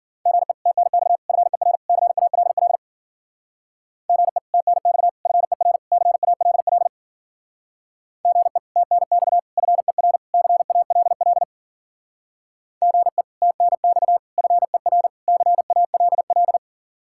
Morse code, heard as a keyed tone, about 49 wpm, 700 Hz. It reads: GE TNX FER CALL GE TNX FER CALL GE TNX FER CALL GE TNX FER CALL
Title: Speed-Racing Common QSO Phrases Encore - 40wpm